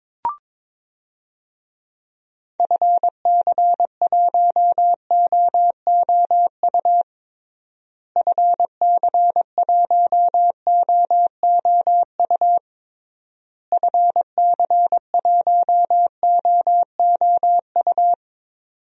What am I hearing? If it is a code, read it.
Morse code, heard as a keyed tone, about 22 wpm, 700 Hz. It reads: FC1OOU FC1OOU FC1OOU
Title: DXCC-100 Call Signs - 6 Characters - Repeated 3x - 22wpm